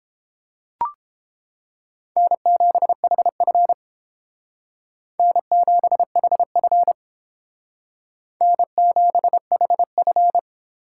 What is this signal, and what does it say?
Morse code, heard as a keyed tone, about 31 wpm, 700 Hz. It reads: N7HF N7HF N7HF